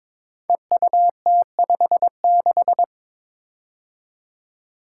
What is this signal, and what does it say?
Morse code, heard as a keyed tone, about 22 wpm, 700 Hz. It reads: EUT56
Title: Code Groups - 5 Letter:Numbers - 22wpm